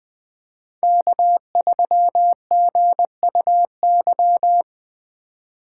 Morse code, heard as a keyed tone, about 20 wpm, 700 Hz. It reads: K3GUY
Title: USA Call Signs - 5 Characters - 4x Character Spacing - 20wpm